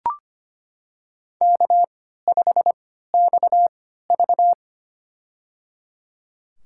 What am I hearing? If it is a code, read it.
Morse code, heard as a keyed tone, about 25 wpm, spaced as for 14 wpm, 700 Hz. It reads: K5XV
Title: USA Call Signs - 3x Character Spacing - 25wpm